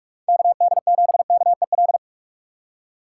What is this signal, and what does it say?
Morse code, heard as a keyed tone, about 46 wpm, 700 Hz. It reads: KD7KEL
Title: USA Call Signs - 6 Characters - 3x Character Spacing - 45wpm